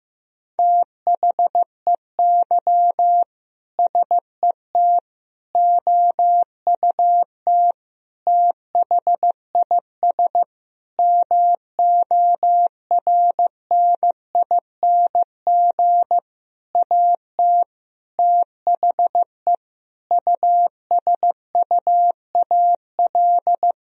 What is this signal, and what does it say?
Morse code, heard as a keyed tone, about 15 wpm, 700 Hz. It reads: THEY SET OUT THIS MORNING AT THE USUAL
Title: Sentences from Top 1000 Words - Review Part 2 - 15wpm